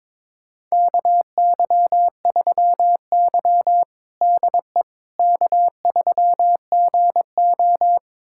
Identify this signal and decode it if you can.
Morse code, heard as a keyed tone, about 22 wpm, 700 Hz. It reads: KY3Y DE K3GO
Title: USA Callsign DE Callsign - 4 Characters - 22wpm